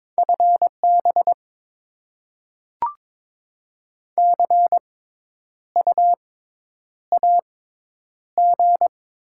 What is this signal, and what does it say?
Morse code, heard as a keyed tone, about 22 wpm, spaced as for 10 wpm, 690 Hz. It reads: FB CU AG
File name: Top 20 QSO Elements - 6x Character Spacing - 22wpm